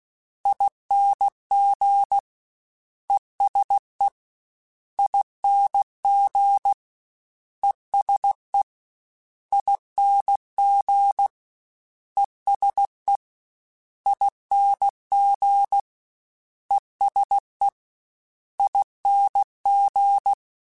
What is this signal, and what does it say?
Morse code, heard as a keyed tone, about 16 wpm, 800 Hz. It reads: ING ESE ING ESE ING ESE ING ESE ING
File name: Special INGs-mp3s / Special INGs-mp3s / ESE ING @15